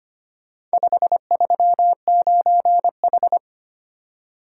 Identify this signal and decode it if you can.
Morse code, heard as a keyed tone, about 25 wpm, 700 Hz. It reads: 539H
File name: practice / 25wpm / Four Letter-Number Combinations - 25wpm